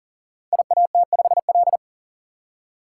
Morse code, heard as a keyed tone, about 40 wpm, 700 Hz. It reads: IAT5L
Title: Code Groups - 5 Letter:Numbers - 40wpm